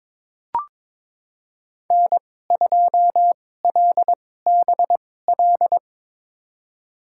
Morse code, heard as a keyed tone, about 22 wpm, spaced as for 16 wpm, 700 Hz. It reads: N2LBL
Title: USA Call Signs - 2x Character Spacing - 22wpm